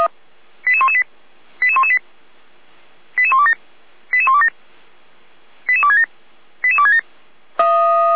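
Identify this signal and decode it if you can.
Five-tone selective call (ZVEI), heard as quick runs of five tones, tones of about 70 ms. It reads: 88118 88118 88127 88127 88267 88267
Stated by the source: ZVEI five-tone call 88288 received on SDR.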